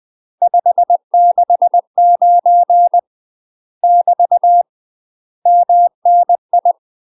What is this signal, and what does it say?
Morse code, heard as a keyed tone, about 20 wpm, 700 Hz. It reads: 569 <BT> MNI
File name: UK MASTER QSOs TRAINING / 20WPM / UK MASTER QSOs 0036